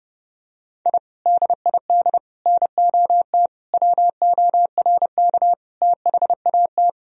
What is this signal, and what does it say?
Morse code, heard as a keyed tone, about 30 wpm, 700 Hz. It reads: I DID NOT WORK THAT
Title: Sentences from Top 1000 Words - Review Part 2 - 30wpm